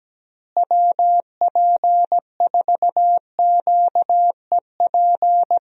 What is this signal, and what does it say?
Morse code, heard as a keyed tone, about 17 wpm, 700 Hz. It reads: WP4QEP
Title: USA Call Signs - 6 Characters - Repeated 3x - 17wpm